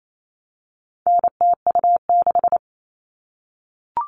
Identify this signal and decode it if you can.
Morse code, heard as a keyed tone, about 28 wpm, 700 Hz. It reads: NTU6
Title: All Cut Numbers - 4-Digit 28wpm